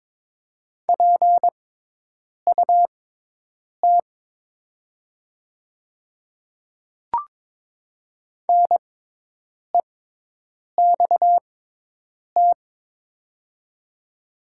Morse code, heard as a keyed tone, about 22 wpm, spaced as for 8 wpm, 700 Hz. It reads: PUT NEXT
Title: 6x Character Spacing - Top 500 Words - 22wpm